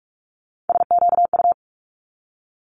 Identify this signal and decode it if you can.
Morse code, heard as a keyed tone, about 46 wpm, 700 Hz. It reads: SQU
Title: Multi-letter Phonemes 45wpm